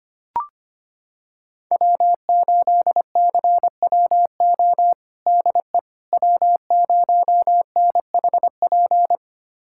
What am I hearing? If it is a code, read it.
Morse code, heard as a keyed tone, about 25 wpm, 700 Hz. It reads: W8CWO DE W0NHP
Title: USA Callsign DE Callsign - 5 Characters - 25wpm